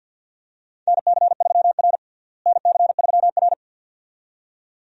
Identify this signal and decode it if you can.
Morse code, heard as a keyed tone, about 52 wpm, 700 Hz. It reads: NC3R NC3R
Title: USA Call Signs - 4 Characters - Repeated 2x - 50wpm